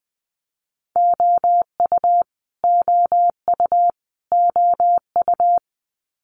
Morse code, heard as a keyed tone, about 20 wpm, 700 Hz. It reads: OU OU OU
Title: Multi-letter Phonemes Repeated 3x 20wpm